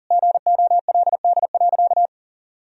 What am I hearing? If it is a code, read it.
Morse code, heard as a keyed tone, about 40 wpm, 700 Hz. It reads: GOLD.
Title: Sentences from Top 1000 Words - Review Part 3 - 40wpm